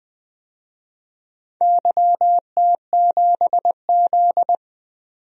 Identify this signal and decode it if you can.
Morse code, heard as a keyed tone, about 20 wpm, 700 Hz. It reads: YT7Z
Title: DXCC-100 Call Signs - 4 Characters - 20wpm